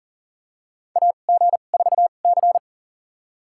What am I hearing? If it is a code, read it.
Morse code, heard as a keyed tone, about 40 wpm, spaced as for 29 wpm, 700 Hz. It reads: AG4C
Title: USA Call Signs - 2x Character Spacing - 40wpm